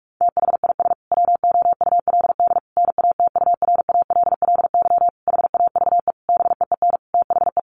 Morse code, heard as a keyed tone, about 44 wpm, 700 Hz. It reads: THIS WOULD NATURALLY HAVE BEEN THE
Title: Sentences from Top 1300 Words 45wpm